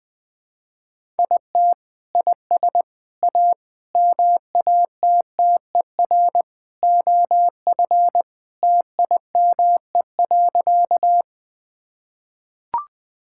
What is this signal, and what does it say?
Morse code, heard as a keyed tone, about 20 wpm, 700 Hz. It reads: IT IS A MATTER OF TIME.
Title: Sentences with 3-Word Phrases - 20wpm